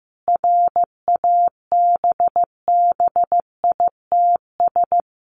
Morse code, heard as a keyed tone, about 15 wpm, 700 Hz. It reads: RABBITS
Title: A Sound Of Thunder 15wpm